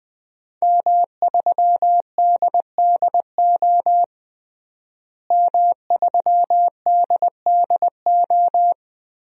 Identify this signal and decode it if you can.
Morse code, heard as a keyed tone, about 20 wpm, 700 Hz. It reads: M3DDO M3DDO
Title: DXCC-100 Call Signs - 5 Characters - Repeated 3x - 20wpm